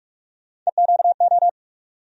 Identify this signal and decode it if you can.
Morse code, heard as a keyed tone, about 47 wpm, 700 Hz. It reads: EQO
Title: USA Call Sign Postfixes - 3 Letters 45wpm